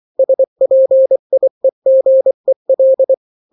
Morse code, heard as a keyed tone, about 24 wpm, 530 Hz. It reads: SPIEGEL
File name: daily / 23wpm